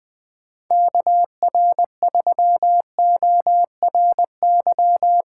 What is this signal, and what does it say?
Morse code, heard as a keyed tone, about 21 wpm, 700 Hz. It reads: KR3ORY
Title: USA Call Signs - 6 Characters - 8x Character Spacing - 20wpm